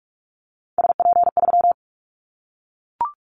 Morse code, heard as a keyed tone, about 44 wpm, 700 Hz. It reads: SP3
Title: Callsign Prefixes for Top 25 DXCC - 45wpm